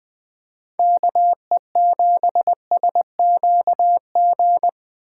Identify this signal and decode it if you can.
Morse code, heard as a keyed tone, about 20 wpm, 700 Hz. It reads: KE7SQG
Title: USA Call Signs - 6 Characters - 6x Character Spacing - 20wpm